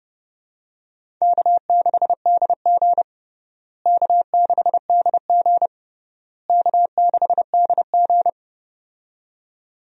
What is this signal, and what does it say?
Morse code, heard as a keyed tone, about 30 wpm, 700 Hz. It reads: K6DG K6DG K6DG
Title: USA Call Signs - 4 Characters - Repeated 3x - 30wpm